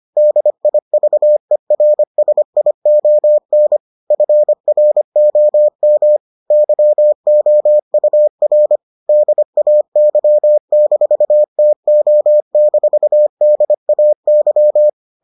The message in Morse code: DIVERSION FROM YOUR DAY-TO-DAY

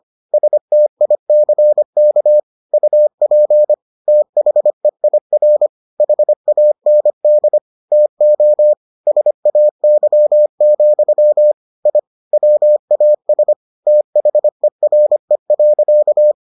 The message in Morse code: STICK UP THEIR HAND TO SAY, I WAS THERE.

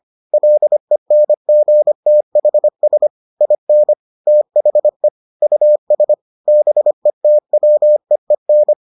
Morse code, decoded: LENGTHS IN THE US BETWEEN

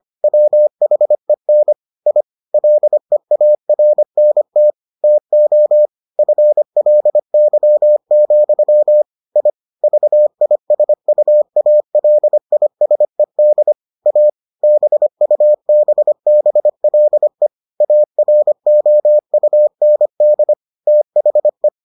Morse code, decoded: WHEN I LEARNT TO FLY, I VISUALISED A BUBBLE AROUND THE

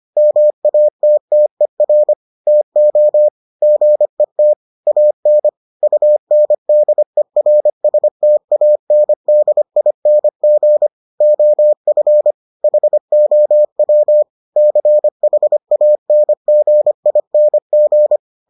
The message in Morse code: MATTER TO GET AN UNDERSTANDING OF HOW CHANGING

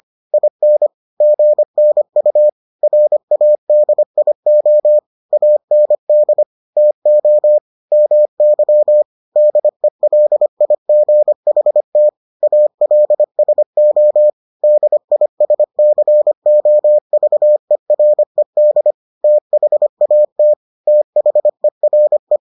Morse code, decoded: IN GNU RADIO AND TO MY DELIGHT ALSO DISCOVERED THAT THERE

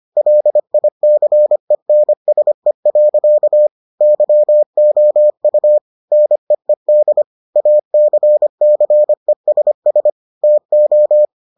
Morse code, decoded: LICENSE. YOU NEED ACCESS TO

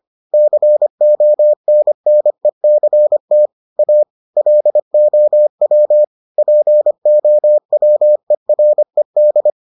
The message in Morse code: CONNECT A LOW POWERED

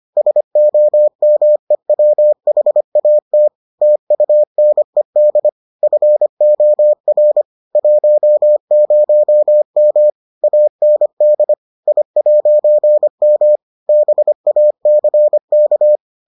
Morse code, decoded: SOMEWHAT TUNED FOR 10M AND I'M BACK

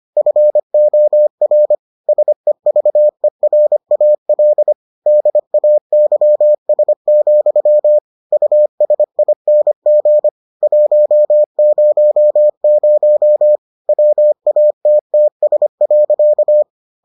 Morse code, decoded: FOR SEVERAL DAYS, USING 100 WATTS.

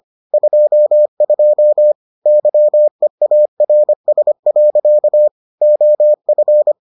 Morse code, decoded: 22 YEARS. OF

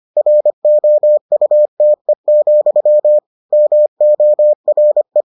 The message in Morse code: ROUTE, MORE

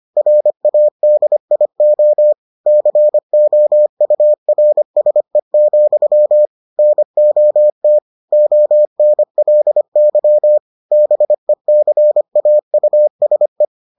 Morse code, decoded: RADIO COURSE, NOT ONLY BECAUSE

